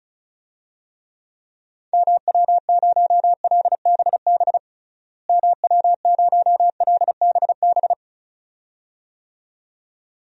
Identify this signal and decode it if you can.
Morse code, heard as a keyed tone, about 34 wpm, 700 Hz. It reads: MW0LBB MW0LBB
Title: DXCC-100 Call Signs - 6 Characters - Repeated 2x - 35wpm